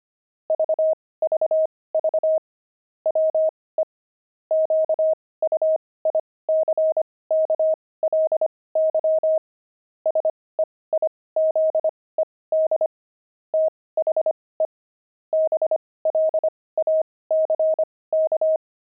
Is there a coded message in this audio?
Yes: Morse code from 0.5 to 18.6 s, heard as a keyed tone, about 25 wpm, spaced as for 18 wpm, 630 Hz. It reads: VVV WE QUICKLY SEIZED THE BLACK